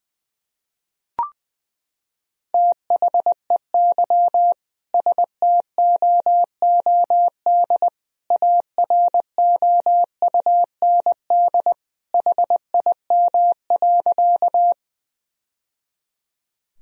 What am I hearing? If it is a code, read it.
Morse code, heard as a keyed tone, about 20 wpm, 700 Hz. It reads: THEY STOOD AROUND HIM.
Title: Sentences from Top 1000 Words - Review Part 3 - 20wpm